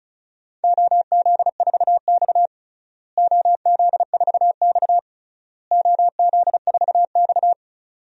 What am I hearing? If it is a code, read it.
Morse code, heard as a keyed tone, about 34 wpm, 700 Hz. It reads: OZ4X OZ4X OZ4X
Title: DXCC-100 Call Signs - 4 Characters - Repeated 3x - 35wpm